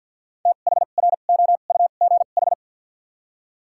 Morse code, heard as a keyed tone, about 53 wpm, spaced as for 37 wpm, 700 Hz. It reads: THROUGH
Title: 2x Character Spacing - Top 200 Words - 50wpm